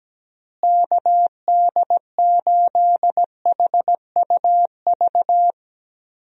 Morse code, heard as a keyed tone, about 17 wpm, 700 Hz. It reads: KD8HUV
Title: USA Call Signs - 6 Characters - Repeated 2x - 17wpm